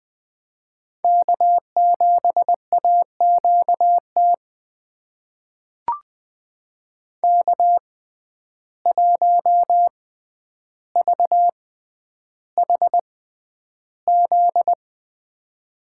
Morse code, heard as a keyed tone, about 20 wpm, spaced as for 10 wpm, 700 Hz. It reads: K7AQT K1VHZ
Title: USA Call Signs - 5 Characters - 6x Character Spacing - 20wpm